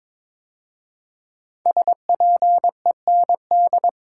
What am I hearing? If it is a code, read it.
Morse code, heard as a keyed tone, about 22 wpm, 700 Hz. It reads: SPEND
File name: 5 Letter Words - First 3 Letters Given - 22wpm